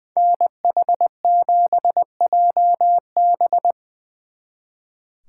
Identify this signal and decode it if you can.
Morse code, heard as a keyed tone, about 23 wpm, 700 Hz. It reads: NH7JB
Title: Speed Racing - USA 2x2 Call Signs 20wpm